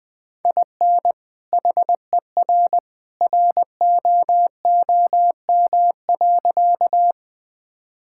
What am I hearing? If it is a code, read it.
Morse code, heard as a keyed tone, about 19 wpm, 700 Hz. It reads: IN HER ROOM.
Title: Sentences with 2-Word Phrases - 20wpm